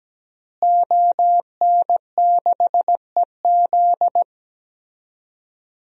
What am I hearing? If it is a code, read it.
Morse code, heard as a keyed tone, about 17 wpm, 700 Hz. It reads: ON6EZ
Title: DXCC-100 Call Signs - 5 Characters - Repeated 3x - 17wpm